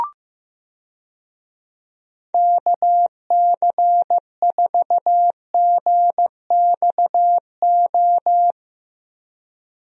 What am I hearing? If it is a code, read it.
Morse code, heard as a keyed tone, about 15 wpm, 700 Hz. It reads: KC4GXO
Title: USA Call Signs - 6 Characters - Repeated 3x - 15wpm